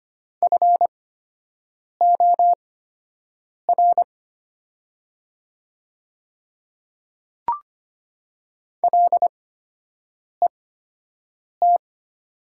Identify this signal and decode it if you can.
Morse code, heard as a keyed tone, about 25 wpm, spaced as for 7 wpm, 700 Hz. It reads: FOR LET
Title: 8x Character Spacing - Top 500 Words - 25wpm